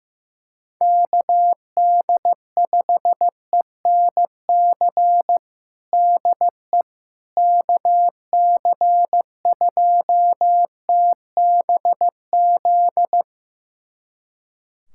Morse code, heard as a keyed tone, about 15 wpm, 700 Hz. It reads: KD5ENC DE KC2TBZ
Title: USA Callsign DE Callsign - 6 Characters - 15wpm